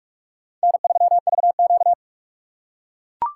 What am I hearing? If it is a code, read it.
Morse code, heard as a keyed tone, about 41 wpm, 700 Hz. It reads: N3VQ
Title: USA Call Signs - 4 Characters - 4x Character Spacing - 45wpm